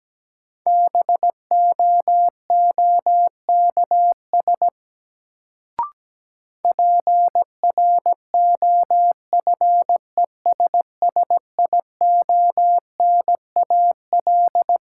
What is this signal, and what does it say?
Morse code, heard as a keyed tone, about 17 wpm, 700 Hz. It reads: BOOKS PROFESSIONAL
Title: Sentences from Top 1000 Words - Review Part 4 - New Word Drill - 17wpm